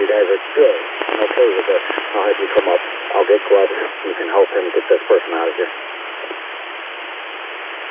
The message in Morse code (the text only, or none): none